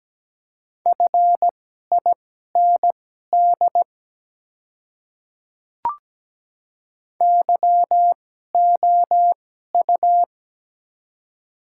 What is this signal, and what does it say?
Morse code, heard as a keyed tone, about 17 wpm, spaced as for 12 wpm, 700 Hz. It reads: FIND YOU
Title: 2x Character Spacing - Top 100 Words - 17wpm